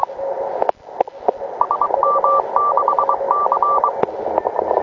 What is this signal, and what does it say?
Morse code, heard as a keyed tone, about 23 wpm, 1170 Hz, with noise. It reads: SM6C
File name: klub / Sat04 3000 VE3ZI